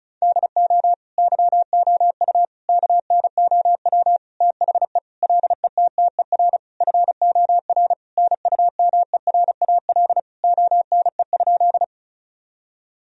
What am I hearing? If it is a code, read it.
Morse code, heard as a keyed tone, about 36 wpm, 700 Hz. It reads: DO YOU KNOW THE LETTER FOR NUMERAL ONE?
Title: Sentences from Top 1000 Words - Review Part 2 - 35wpm